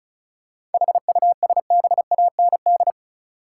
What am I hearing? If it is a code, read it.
Morse code, heard as a keyed tone, about 34 wpm, 700 Hz. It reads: HUSBAND
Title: Seven Letter Words - 35wpm